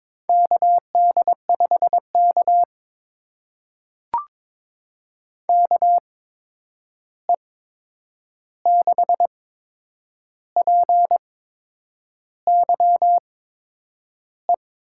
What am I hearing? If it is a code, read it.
Morse code, heard as a keyed tone, about 22 wpm, spaced as for 8 wpm, 700 Hz. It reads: KD5K KE6PYE
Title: USA Call Signs - 8x Character Spacing - 22wpm